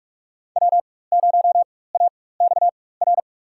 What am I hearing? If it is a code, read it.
Morse code, heard as a keyed tone, about 47 wpm, spaced as for 21 wpm, 700 Hz. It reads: W0AXR
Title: USA Call Signs - 5 Characters - 4x Character Spacing - 45wpm